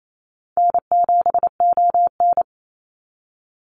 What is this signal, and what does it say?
Morse code, heard as a keyed tone, about 28 wpm, 700 Hz. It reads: N7ON
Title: Easy Call Signs - 28wpm